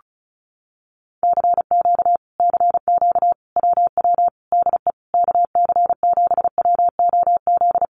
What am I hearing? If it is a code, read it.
Morse code, heard as a keyed tone, about 34 wpm, 700 Hz. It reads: CQ CQ WW DE KC7WOZ
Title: Common QSO Phrases Encore 35wpm